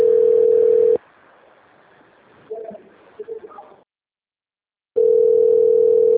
Ringback tone (North America), 440 + 480 Hz.